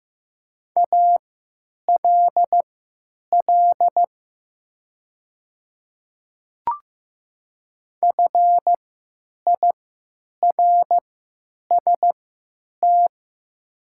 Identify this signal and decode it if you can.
Morse code, heard as a keyed tone, about 15 wpm, spaced as for 9 wpm, 700 Hz. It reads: ALL FIRST
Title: 3x Character Spacing - Top 100 Words - 15wpm